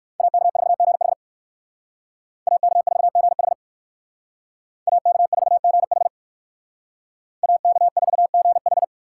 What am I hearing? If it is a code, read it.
Morse code, heard as a keyed tone, about 54 wpm, 700 Hz. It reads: AK4GH AK4GH AK4GH AK4GH
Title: Speed Racing - USA 2x2 Call Signs 45wpm